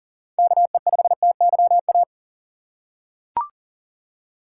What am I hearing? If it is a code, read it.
Morse code, heard as a keyed tone, about 40 wpm, 700 Hz. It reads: KE5TYA
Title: USA Call Signs - 6 Characters - 4x Character Spacing - 40wpm